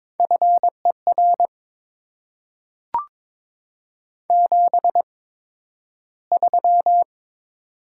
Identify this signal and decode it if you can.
Morse code, heard as a keyed tone, about 22 wpm, spaced as for 12 wpm, 700 Hz. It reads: FER 73
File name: Top 20 QSO Elements - 8x Character Spacing - 22wpm